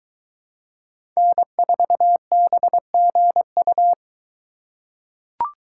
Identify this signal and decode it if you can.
Morse code, heard as a keyed tone, about 22 wpm, 700 Hz. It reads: N4BGU